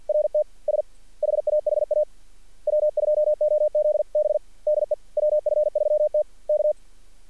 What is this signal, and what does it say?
Morse code, heard as a keyed tone, about 46 wpm, 600 Hz. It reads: RT S HAHA W2OZB DE WF3T K
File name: archived / CRSnet